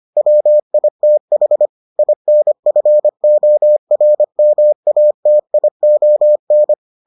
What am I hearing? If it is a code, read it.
Morse code, heard as a keyed tone, about 25 wpm, 600 Hz. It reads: WITH INFORMATION